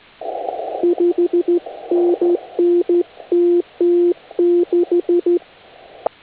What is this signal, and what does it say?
Morse code, heard as a keyed tone, about 13 wpm, 350 Hz. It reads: 5NN TT6
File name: HA6A CW